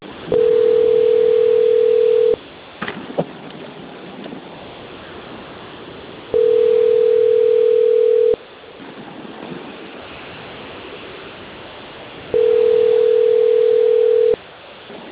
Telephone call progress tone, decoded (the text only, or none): ringback tone